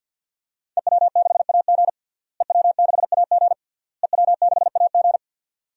Morse code, heard as a keyed tone, about 47 wpm, 700 Hz. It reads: EW6AG EW6AG EW6AG
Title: DXCC-100 Call Signs - 5 Characters - Repeated 3x - 50wpm